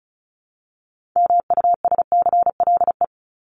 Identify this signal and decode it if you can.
Morse code, heard as a keyed tone, about 34 wpm, 700 Hz. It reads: MUSCLE
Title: Top 1251-1500 Words 35wpm